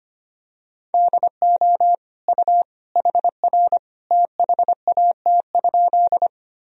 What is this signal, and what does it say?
Morse code, heard as a keyed tone, about 25 wpm, 700 Hz. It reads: DO U HR THAT?
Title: Sentences from Top 300 Words and Abbreviations 25wpm